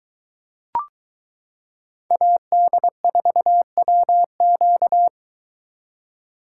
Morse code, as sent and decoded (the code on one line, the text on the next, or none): .- -.. ....- .-- --.-
AD4WQ